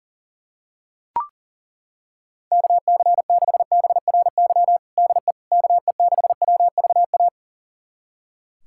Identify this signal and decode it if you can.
Morse code, heard as a keyed tone, about 40 wpm, 700 Hz. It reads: KC6BRY DE KE6WVA
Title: USA Callsign DE Callsign - 6 Characters - 40wpm